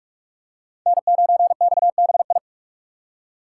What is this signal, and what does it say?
Morse code, heard as a keyed tone, about 44 wpm, 700 Hz. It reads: N9XBI